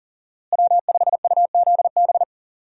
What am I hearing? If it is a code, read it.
Morse code, heard as a keyed tone, about 39 wpm, 700 Hz. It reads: W5UZB
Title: USA Call Signs - 5 Characters - Repeated 3x - 40wpm